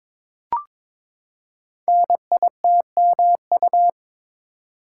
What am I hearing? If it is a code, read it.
Morse code, heard as a keyed tone, about 22 wpm, 700 Hz. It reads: NITMU
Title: Code Groups - 5 Letter:Numbers - 22wpm